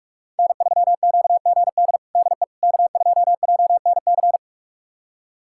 Morse code, heard as a keyed tone, about 41 wpm, 700 Hz. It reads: N3QGD DE K2JNC
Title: USA Callsign DE Callsign - 5 Characters - 45wpm